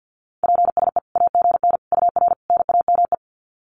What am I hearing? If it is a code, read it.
Morse code, heard as a keyed tone, about 50 wpm, 700 Hz. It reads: PSE AGN UR NAME
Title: Common QSO Phrases 50wpm